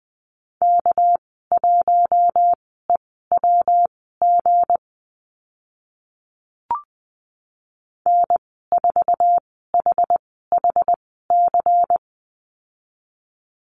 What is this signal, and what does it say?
Morse code, heard as a keyed tone, about 19 wpm, spaced as for 14 wpm, 700 Hz. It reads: K1EWG N4HHC
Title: 200 Lesson 32g - Letter k - Call Signs 20wpm